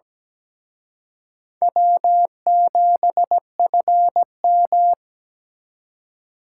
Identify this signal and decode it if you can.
Morse code, heard as a keyed tone, about 20 wpm, 700 Hz. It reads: W7FM
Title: Speed Racing - USA 1x2 Call Signs 17wpm